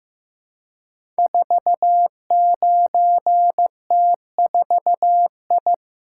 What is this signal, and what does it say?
Morse code, heard as a keyed tone, about 15 wpm, 700 Hz. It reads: 49T4I
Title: Code Groups - 5 Letter:Numbers - 15wpm